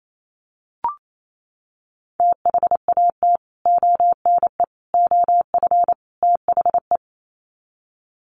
Morse code, heard as a keyed tone, about 28 wpm, 700 Hz. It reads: THAT ONE OF THE
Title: Sets of 4 Words 28wpm